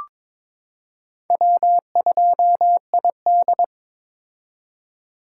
Morse code, heard as a keyed tone, about 22 wpm, 700 Hz. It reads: W2ID
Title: K1USN SST QSO Practice - 22wpm